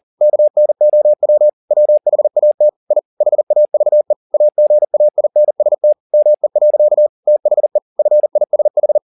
Morse code, heard as a keyed tone, about 40 wpm, 600 Hz. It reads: KNOW WHAT I HAVE AGAINST ME. THE FISH